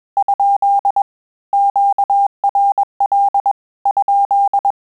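Morse code, heard as a keyed tone, about 21 wpm, 800 Hz. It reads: ? QRL ?
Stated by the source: Morse QSO practice at 20 words per minute, in Italian and English.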